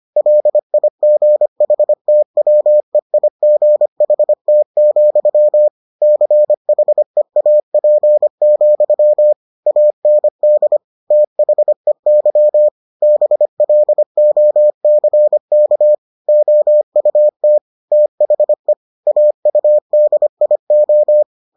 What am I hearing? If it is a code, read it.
Morse code, heard as a keyed tone, about 25 wpm, 600 Hz. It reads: LIGHTWEIGHT, CHEAP, AND THEY BLOCK OUT THE AUDIO